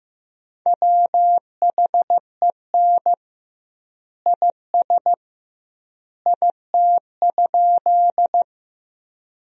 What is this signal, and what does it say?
Morse code, heard as a keyed tone, about 15 wpm, 700 Hz. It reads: WHEN IS IT?
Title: 2x Word-Spacing - Sentences from Top 100 Words - 15wpm